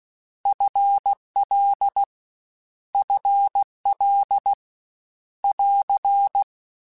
Morse code, heard as a keyed tone, about 16 wpm, 800 Hz. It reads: FL FL <AR>
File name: Contrasts / Contrasts-mp3s / 2LtrContrasts @15